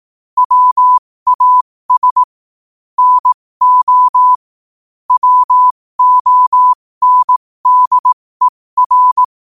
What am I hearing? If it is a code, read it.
Morse code, heard as a keyed tone, about 18 wpm, spaced as for 15 wpm, 1000 Hz. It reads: WAS NO WONDER
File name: qotdfiles / 15wpm